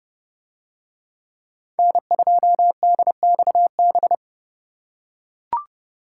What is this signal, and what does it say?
Morse code, heard as a keyed tone, about 29 wpm, 700 Hz. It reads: N2DXB